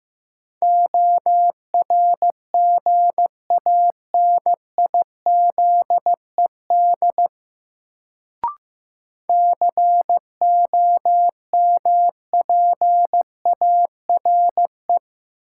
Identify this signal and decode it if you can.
Morse code, heard as a keyed tone, about 15 wpm, 700 Hz. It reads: ORGANIZED COMPARE
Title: Sentences from Top 1300 Words - New Word Drill - 15wpm